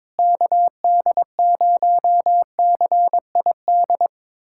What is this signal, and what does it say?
Morse code, heard as a keyed tone, about 22 wpm, 700 Hz. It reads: KD0CID
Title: USA Callsign DE Callsign 22wpm